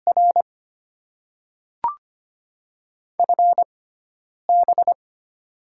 Morse code, heard as a keyed tone, about 25 wpm, spaced as for 9 wpm, 700 Hz. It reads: R FB
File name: Top 20 QSO Elements - 6x Character Spacing - 25wpm